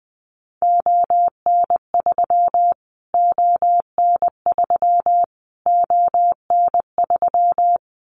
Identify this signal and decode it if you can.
Morse code, heard as a keyed tone, about 20 wpm, 700 Hz. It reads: ON3 ON3 ON3
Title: Callsign Prefixes for Top 25 DXCC - Repeated 3x - 20wpm